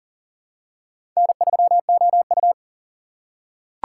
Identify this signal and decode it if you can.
Morse code, heard as a keyed tone, about 40 wpm, 700 Hz. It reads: N3OU
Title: USA Call Signs - 4 Characters - 4x Character Spacing - 40wpm